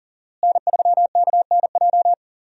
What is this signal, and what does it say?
Morse code, heard as a keyed tone, about 37 wpm, 700 Hz. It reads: N3KNJ
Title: USA Call Signs - 5 Characters - Repeated 2x - 40wpm